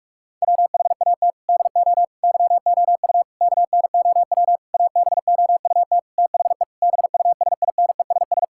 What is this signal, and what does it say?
Morse code, heard as a keyed tone, about 46 wpm, 700 Hz. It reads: WHAT DO YOU KNOW ABOUT THE BUSINESS
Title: Sentences from Top 1000 Words - Review Part 6 - 45wpm